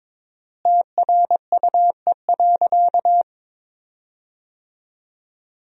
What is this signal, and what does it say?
Morse code, heard as a keyed tone, about 22 wpm, 700 Hz. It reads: TRUE.
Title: 4x Word-Spacing - Sentences from Top 400 Words - 22wpm